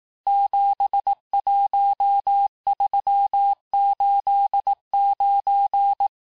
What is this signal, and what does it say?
Morse code, heard as a keyed tone, about 18 wpm, 800 Hz. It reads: 71389
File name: cours-du-lundi files / cours-1594--teste-chiffre-vitesse-17mpm